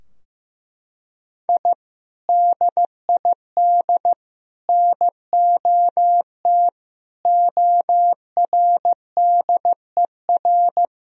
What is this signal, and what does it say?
Morse code, heard as a keyed tone, about 16 wpm, 700 Hz. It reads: I DID NOT ORDER
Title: Sentences from Top 1000 Words - Review Part 2 - 15wpm